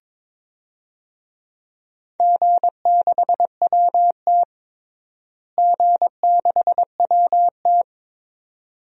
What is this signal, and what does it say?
Morse code, heard as a keyed tone, about 22 wpm, 700 Hz. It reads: G6WT G6WT
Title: DXCC-100 Call Signs - 4 Characters - Repeated 2x - 22wpm